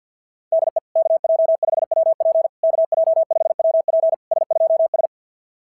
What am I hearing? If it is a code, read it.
Morse code, heard as a keyed tone, about 50 wpm, 650 Hz. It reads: DE KJ5WP KJ5WP S2S
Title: SOTA QSO Practice 50wpm